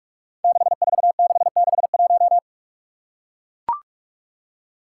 Morse code, heard as a keyed tone, about 42 wpm, 700 Hz. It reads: 64661
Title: Five Challenging Letter-Number Combinations 45wpm